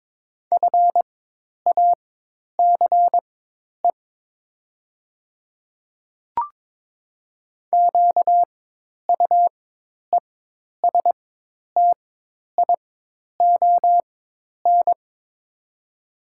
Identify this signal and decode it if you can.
Morse code, heard as a keyed tone, about 22 wpm, spaced as for 10 wpm, 700 Hz. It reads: FACE QUESTION
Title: 4x Character Spacing - Top 500 Words - 22wpm